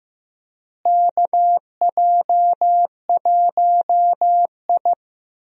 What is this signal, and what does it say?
Morse code, heard as a keyed tone, about 15 wpm, 700 Hz. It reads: KJ1I
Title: USA Call Signs - 4 Characters - Repeated 3x - 15wpm